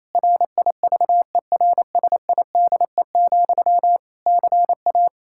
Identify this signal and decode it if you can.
Morse code, heard as a keyed tone, about 27 wpm, 700 Hz. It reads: RIVERSIDE, CA
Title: Practice QSO - Exchange 1 - Beginning - 28wpm